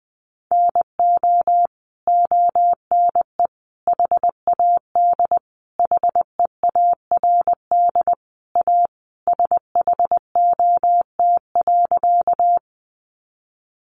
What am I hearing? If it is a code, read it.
Morse code, heard as a keyed tone, about 20 wpm, 700 Hz. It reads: NO ONE HAD HEARD A SHOT.
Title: Sentences from Top 1100 Words 20wpm